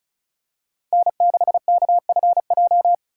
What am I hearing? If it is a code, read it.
Morse code, heard as a keyed tone, about 34 wpm, 700 Hz. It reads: N6KFJ
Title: USA Call Signs - 5 Characters - 4x Character Spacing - 35wpm